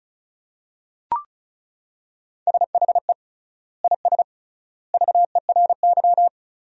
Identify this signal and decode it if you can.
Morse code, heard as a keyed tone, about 36 wpm, 700 Hz. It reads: SHE IS VERY
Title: 3x Word-Spacing - Sentences from Top 400 Words - 35wpm